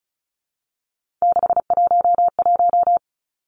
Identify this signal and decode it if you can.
Morse code, heard as a keyed tone, about 35 wpm, 700 Hz. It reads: 611
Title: Three Challenging Letter-Number Combinations 35wpm